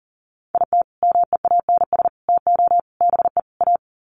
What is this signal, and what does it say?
Morse code, heard as a keyed tone, about 40 wpm, 700 Hz. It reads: IT MEANS TO BE A
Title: Sets of 5 Words Encore2 40wpm